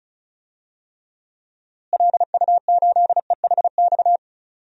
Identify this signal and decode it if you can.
Morse code, heard as a keyed tone, about 34 wpm, 700 Hz. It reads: LU8EHX
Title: DXCC-100 Call Signs - 6 Characters - Repeated 3x - 35wpm